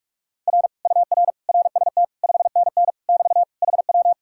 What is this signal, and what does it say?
Morse code, heard as a keyed tone, about 45 wpm, 700 Hz. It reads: R UR RST 5NN <BT> HW